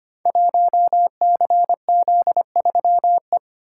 Morse code, heard as a keyed tone, about 25 wpm, 700 Hz. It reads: 1CZ3E
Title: Code Groups - 5 Letter:Numbers - 25wpm